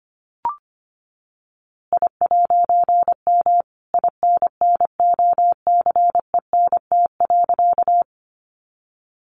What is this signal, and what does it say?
Morse code, heard as a keyed tone, about 25 wpm, 700 Hz. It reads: I'M INNOCENT.
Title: A Sound Of Thunder 25wpm